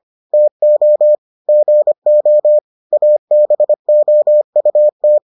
Morse code, heard as a keyed tone, about 25 wpm, 600 Hz. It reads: TO GO ABOUT